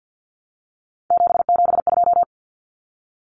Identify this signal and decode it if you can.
Morse code, heard as a keyed tone, about 50 wpm, 700 Hz. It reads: 772
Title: Numbers - 3-Digit 50wpm